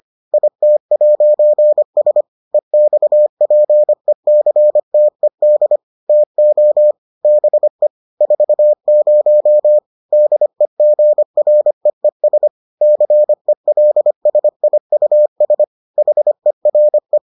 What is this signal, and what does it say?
Morse code, heard as a keyed tone, about 25 wpm, 600 Hz. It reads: IT'S EXPECTED TO BE 40 DEGREES CELSIUS HERE